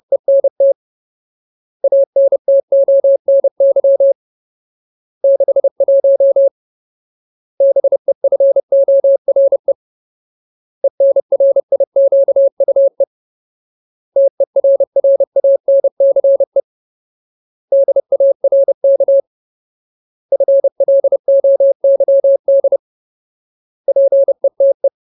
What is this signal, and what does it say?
Morse code, heard as a keyed tone, about 30 wpm, 550 Hz. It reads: ENT ANTONY 61 BEFORE ENRIQUE TERRANCE DARK FLOYD PETE
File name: Ultimate Word List / 30WPM Ultimate